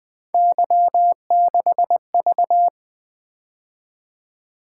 Morse code, heard as a keyed tone, about 20 wpm, 700 Hz. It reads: Y6V
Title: Three Letter-Number Combinations - 20wpm